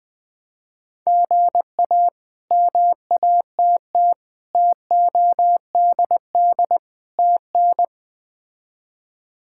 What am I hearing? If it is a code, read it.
Morse code, heard as a keyed tone, about 20 wpm, 700 Hz. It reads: GA MATT TODD TN
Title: K1USN SST QSO Practice - 20wpm